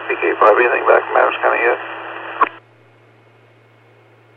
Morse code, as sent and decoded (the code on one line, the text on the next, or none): none